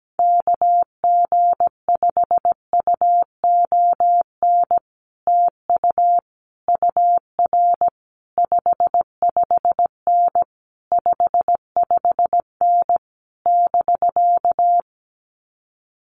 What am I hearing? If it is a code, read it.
Morse code, heard as a keyed tone, about 17 wpm, 700 Hz. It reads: KG5UON TU UR 55N 55N <BK>
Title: Common QSO Phrases Encore 17wpm